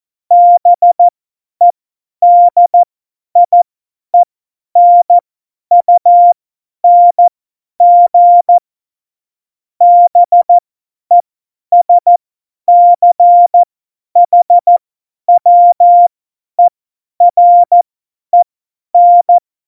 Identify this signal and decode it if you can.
Morse code, heard as a keyed tone, about 14 wpm, spaced as for 10 wpm, 700 Hz. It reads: BEDIENUNG BESCHWEREN
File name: Morsekurs / Witze / Witz 17.Dezember 2019, 50 F70